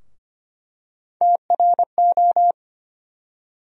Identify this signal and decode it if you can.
Morse code, heard as a keyed tone, about 25 wpm, 700 Hz. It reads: TRO
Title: USA Call Sign Postfixes - 3 Letters 25wpm